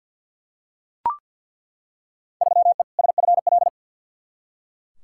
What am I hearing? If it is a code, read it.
Morse code, heard as a keyed tone, about 47 wpm, 700 Hz. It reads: 3E SFL
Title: Field Day Exchanges Encore - 50wpm